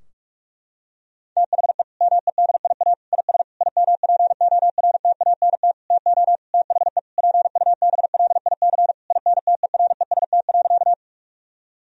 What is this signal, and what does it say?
Morse code, heard as a keyed tone, about 46 wpm, 700 Hz. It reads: THE MEDIA IS IMPORTANT TO THE PUBLIC INTEREST.
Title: Sentences from Top 1000 Words - Review Part 4 - 45wpm